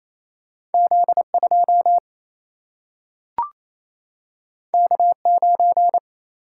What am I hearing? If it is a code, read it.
Morse code, heard as a keyed tone, about 28 wpm, 700 Hz. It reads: Z 2 K 9
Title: Single-Letter Single-Number 28wpm